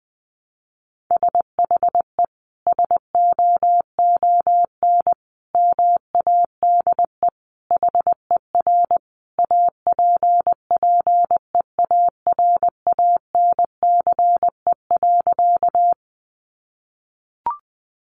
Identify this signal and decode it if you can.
Morse code, heard as a keyed tone, about 20 wpm, 700 Hz. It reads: SHE SOON MADE HER APPEARANCE.
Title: Sentences from Top 1100 Words 20wpm